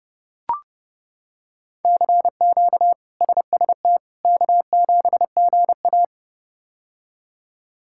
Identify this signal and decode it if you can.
Morse code, heard as a keyed tone, about 30 wpm, 700 Hz. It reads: CQ SST K7GA
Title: K1USN SST QSO Practice - 30wpm